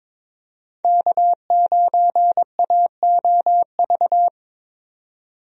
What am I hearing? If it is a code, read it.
Morse code, heard as a keyed tone, about 22 wpm, 700 Hz. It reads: K9AOV
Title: USA Call Signs - 5 Characters - 22wpm